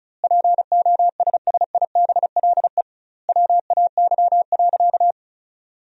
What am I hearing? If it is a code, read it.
Morse code, heard as a keyed tone, about 36 wpm, 700 Hz. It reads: POSSIBLE WAY.
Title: 2x Word-Spacing - Sentences from Top 500 Words - 35wpm